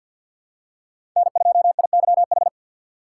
Morse code, heard as a keyed tone, about 47 wpm, 700 Hz. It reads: N2IYH
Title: USA Call Signs - 5 Characters - 4x Character Spacing - 50wpm